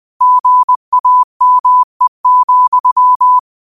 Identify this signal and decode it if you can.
Morse code, heard as a keyed tone, about 20 wpm, 1000 Hz. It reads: GAME,
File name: qotdfiles / 20wpm